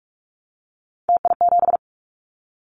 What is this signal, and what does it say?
Morse code, heard as a keyed tone, about 44 wpm, 700 Hz. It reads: TI7
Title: Callsign Prefixes for Top 100 DXCC - 45wpm